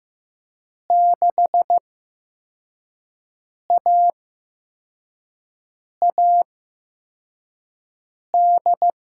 Morse code, heard as a keyed tone, about 15 wpm, spaced as for 4 wpm, 700 Hz. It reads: 6AAD
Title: USA Call Signs - 5 Characters - 8x Character Spacing - 15wpm